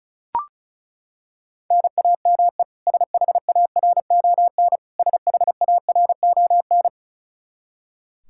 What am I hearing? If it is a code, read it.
Morse code, heard as a keyed tone, about 36 wpm, 700 Hz. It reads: NAME SHARON SHARON
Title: Practice QSO - Exchange 1 - Beginning - 35wpm